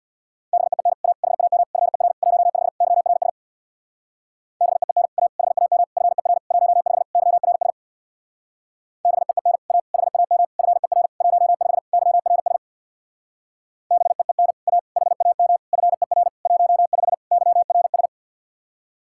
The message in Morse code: BEEN A HAM FER 15 YRS BEEN A HAM FER 15 YRS BEEN A HAM FER 15 YRS BEEN A HAM FER 15 YRS

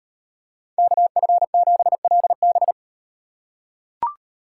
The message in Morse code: KF7LB

